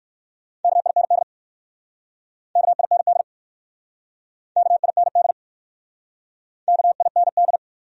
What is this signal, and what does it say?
Morse code, heard as a keyed tone, about 54 wpm, 700 Hz. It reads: KIND KIND KIND KIND
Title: Speed-Racing Top 200 Words 45wpm